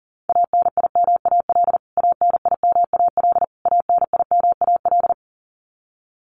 Morse code, heard as a keyed tone, about 40 wpm, 700 Hz. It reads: ANIMAL ANIMAL ANIMAL
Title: Top 200 Words Repeated 40wpm